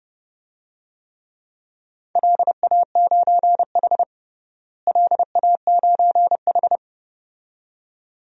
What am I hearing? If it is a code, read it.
Morse code, heard as a keyed tone, about 30 wpm, 700 Hz. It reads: LA9H LA9H
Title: DXCC-100 Call Signs - 4 Characters - Repeated 2x - 30wpm